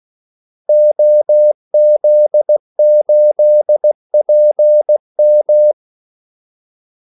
Morse code, heard as a keyed tone, about 16 wpm, 600 Hz. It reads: OZ8PM
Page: Kaldesignaler 16 wpm